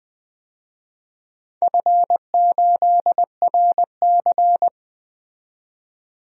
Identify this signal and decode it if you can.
Morse code, heard as a keyed tone, about 21 wpm, 700 Hz. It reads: F8RC
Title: DXCC-100 Call Signs - 4 Characters - 20wpm